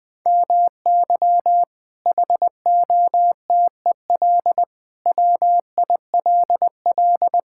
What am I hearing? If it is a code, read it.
Morse code, heard as a keyed tone, about 20 wpm, 700 Hz. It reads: MY HOTEL WILL
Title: Sentences from Top 1600 Words 20wpm